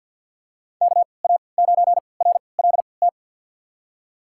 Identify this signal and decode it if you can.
Morse code, heard as a keyed tone, about 52 wpm, spaced as for 29 wpm, 700 Hz. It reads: KA9RLT
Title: USA Call Signs - 6 Characters - 3x Character Spacing - 50wpm